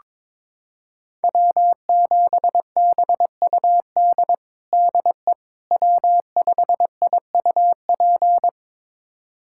Morse code, heard as a keyed tone, about 21 wpm, 700 Hz. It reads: W7BUD DE W5IUP
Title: USA Callsign DE Callsign - 5 Characters - 22wpm